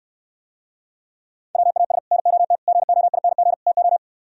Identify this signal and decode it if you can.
Morse code, heard as a keyed tone, about 67 wpm, 700 Hz. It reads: WAS NOT GOING TO